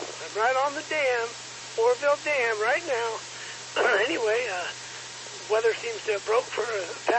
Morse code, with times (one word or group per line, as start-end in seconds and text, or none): none